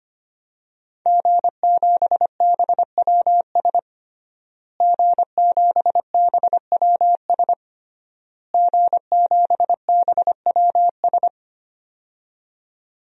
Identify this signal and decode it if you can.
Morse code, heard as a keyed tone, about 25 wpm, 700 Hz. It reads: G7BWS G7BWS G7BWS
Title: DXCC-100 Call Signs - 5 Characters - Repeated 3x - 25wpm